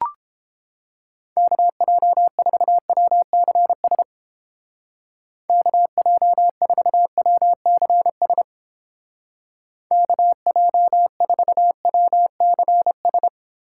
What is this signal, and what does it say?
Morse code, heard as a keyed tone, about 31 wpm, 700 Hz. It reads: KJ4WCS KJ4WCS KJ4WCS